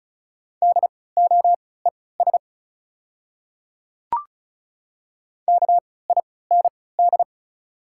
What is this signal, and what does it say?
Morse code, heard as a keyed tone, about 36 wpm, spaced as for 20 wpm, 700 Hz. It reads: DOES KIND
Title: 3x Character Spacing - Top 200 Words - 35wpm